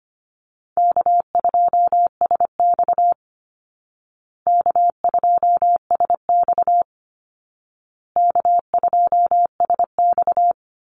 Morse code, heard as a keyed tone, about 25 wpm, 700 Hz. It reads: K2SX K2SX K2SX
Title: Call Signs Repeated 2x - 25wpm